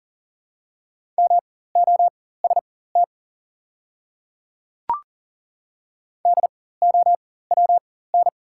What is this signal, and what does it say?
Morse code, heard as a keyed tone, about 40 wpm, spaced as for 19 wpm, 700 Hz. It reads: MOST DOWN
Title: 4x Character Spacing - Top 200 Words - 40wpm